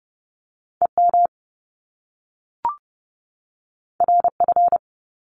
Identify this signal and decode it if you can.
Morse code, heard as a keyed tone, about 30 wpm, 700 Hz. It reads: EM RF
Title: Top Two Letter Combinations in Words 30wpm